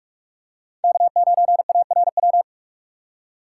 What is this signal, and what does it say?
Morse code, heard as a keyed tone, about 46 wpm, 700 Hz. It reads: K9ARW
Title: USA Call Signs - 5 Characters - 3x Character Spacing - 45wpm